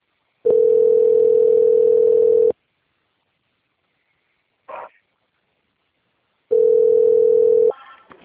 Ringback tone (North America), 440 + 480 Hz.